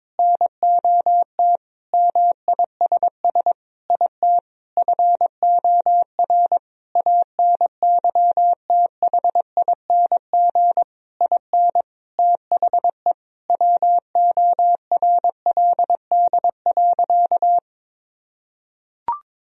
Morse code, heard as a keyed tone, about 22 wpm, 700 Hz. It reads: NOT MISS IT FOR ANYTHING IN THE WORLD.
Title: Sentences from Top 1000 Words - Review Part 3 - 22wpm